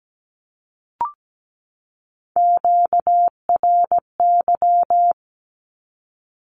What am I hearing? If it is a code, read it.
Morse code, heard as a keyed tone, about 17 wpm, 700 Hz. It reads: QRY
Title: Q-Codes 17wpm